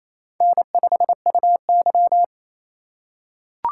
Morse code, heard as a keyed tone, about 28 wpm, 700 Hz. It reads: N5UY
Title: USA Call Signs - 4 Characters - 4x Character Spacing - 28wpm